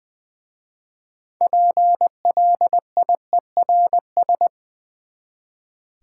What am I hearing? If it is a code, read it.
Morse code, heard as a keyed tone, about 20 wpm, 700 Hz. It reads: PLIERS
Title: Easy Words with Hints - 20wpm